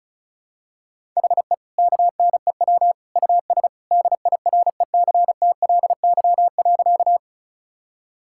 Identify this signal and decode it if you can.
Morse code, heard as a keyed tone, about 36 wpm, 700 Hz. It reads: HE KNEW US DIRECTLY.
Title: Sentences from Top 1000 Words - Review Part 5 - 35wpm